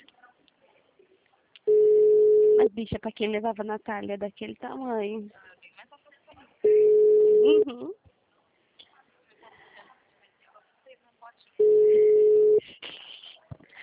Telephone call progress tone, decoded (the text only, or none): ringback tone